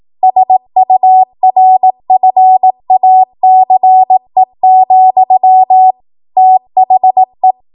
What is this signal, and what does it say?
Morse code, heard as a keyed tone, about 18 wpm, 750 Hz. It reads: SURFACE, THE
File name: Archive / 18 WPM / 18WPM